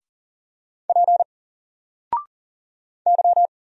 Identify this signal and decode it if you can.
Morse code, heard as a keyed tone, about 39 wpm, 700 Hz. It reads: P Y
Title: Single Character (Letter:Number:Punctuation:Prosign) 40wpm